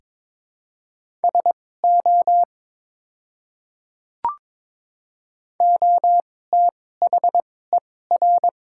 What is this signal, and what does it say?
Morse code, heard as a keyed tone, about 22 wpm, spaced as for 16 wpm, 700 Hz. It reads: SO OTHER
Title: 2x Character Spacing - Top 100 Words - 22wpm